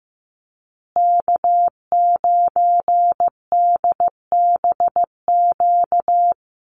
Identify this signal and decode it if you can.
Morse code, heard as a keyed tone, about 15 wpm, 700 Hz. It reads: K9DBQ
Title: USA Call Signs Repeated 3x 15wpm